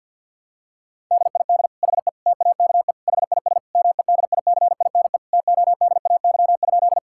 Morse code, heard as a keyed tone, about 51 wpm, 700 Hz. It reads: DID HE TAKE HIS MEDICINE TODAY?
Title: Sentences from Top 1000 Words - Review Part 6 - 50wpm